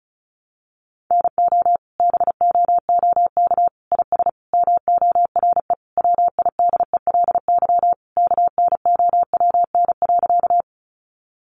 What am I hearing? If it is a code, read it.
Morse code, heard as a keyed tone, about 34 wpm, 700 Hz. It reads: NO BOOK IS MORE WIDELY KNOWN.
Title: Sentences from Top 1200 Words 35wpm